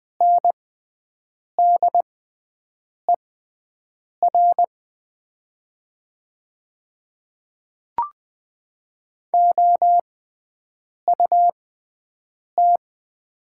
Morse code, heard as a keyed tone, about 20 wpm, spaced as for 7 wpm, 700 Hz. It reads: NDER OUT
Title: 6x Character Spacing - Top 200 Words - 20wpm